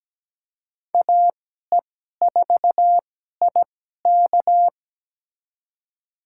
Morse code, heard as a keyed tone, about 17 wpm, spaced as for 12 wpm, 700 Hz. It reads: AE4IK
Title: USA Call Signs - 5 Characters - 2x Character Spacing - 17wpm